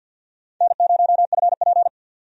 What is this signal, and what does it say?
Morse code, heard as a keyed tone, about 50 wpm, 700 Hz. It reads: N0FP
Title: USA Call Signs - 4 Characters - 50wpm